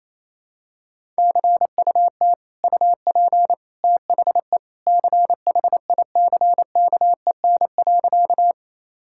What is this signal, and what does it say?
Morse code, heard as a keyed tone, about 28 wpm, 700 Hz. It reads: CUT UP THE CHICKEN.
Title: Sentences from Top 1000 Words - Review Part 5 - 28wpm